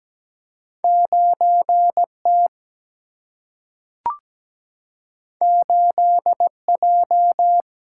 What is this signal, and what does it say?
Morse code, heard as a keyed tone, about 17 wpm, 700 Hz. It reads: 9T 8J
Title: Single-Letter Single-Number 17wpm